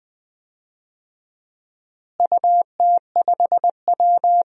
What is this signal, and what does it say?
Morse code, heard as a keyed tone, about 20 wpm, 700 Hz. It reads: UT5W
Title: DXCC-100 Call Signs - 4 Characters - Repeated 2x - 20wpm